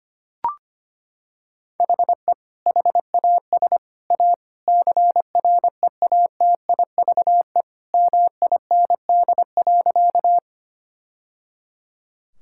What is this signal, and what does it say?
Morse code, heard as a keyed tone, about 25 wpm, 700 Hz. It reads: HE HAS A CREATIVE MIND.
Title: Sentences from Top 1000 Words - Review Part 5 - 25wpm